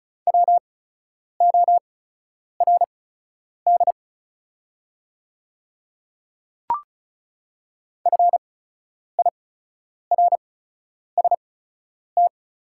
Morse code, heard as a keyed tone, about 36 wpm, spaced as for 10 wpm, 700 Hz. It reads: WORD FIRST
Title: 8x Character Spacing - Top 100 Words - 35wpm